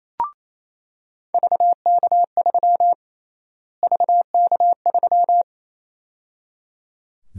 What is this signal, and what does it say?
Morse code, heard as a keyed tone, about 28 wpm, 700 Hz. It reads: VK3 VK3
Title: USA Call Sign Prefixes - Repeated 2x - 28wpm